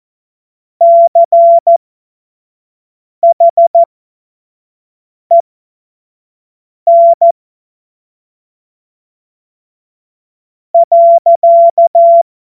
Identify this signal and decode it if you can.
Morse code, heard as a keyed tone, about 14 wpm, spaced as for 5 wpm, 680 Hz. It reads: CHEN .
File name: Morsekurs / Witze / Witz 21. Dezember 2021, 50 F70